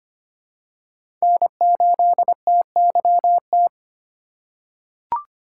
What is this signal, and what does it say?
Morse code, heard as a keyed tone, about 25 wpm, 700 Hz. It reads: N8TYT
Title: USA Call Signs - 5 Characters - 6x Character Spacing - 25wpm